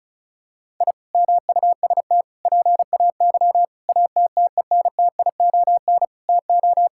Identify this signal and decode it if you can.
Morse code, heard as a keyed tone, about 36 wpm, 700 Hz. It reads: I MUST PAY ATTENTION TO
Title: Sentences from Top 1000 Words - Review Part 6 - 35wpm